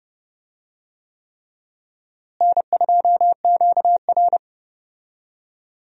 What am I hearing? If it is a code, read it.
Morse code, heard as a keyed tone, about 32 wpm, 700 Hz. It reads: N2QR
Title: USA Call Signs - 4 Characters - Repeated 3x - 30wpm